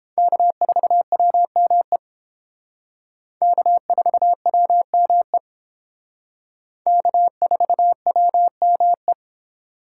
Morse code, heard as a keyed tone, about 28 wpm, 700 Hz. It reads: K4WME K4WME K4WME